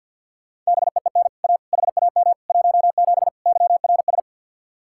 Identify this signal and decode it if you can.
Morse code, heard as a keyed tone, about 63 wpm, 700 Hz. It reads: BEEN A HAM 17 YRS